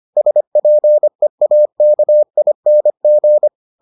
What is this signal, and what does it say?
Morse code, heard as a keyed tone, about 25 wpm, 600 Hz. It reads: SPEAKING